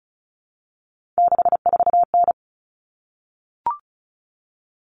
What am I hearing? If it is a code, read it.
Morse code, heard as a keyed tone, about 34 wpm, 700 Hz. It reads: 64N
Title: All Cut Numbers - 3-Digit 35wpm